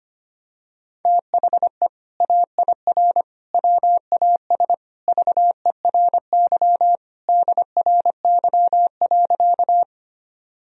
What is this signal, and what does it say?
Morse code, heard as a keyed tone, about 25 wpm, 700 Hz. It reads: THE AIR WAS VERY DRY.
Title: Sentences from Top 1000 Words - Review Part 3 - 25wpm